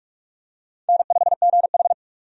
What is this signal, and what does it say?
Morse code, heard as a keyed tone, about 46 wpm, 700 Hz. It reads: N5GH
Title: USA Call Signs - 4 Characters - 45wpm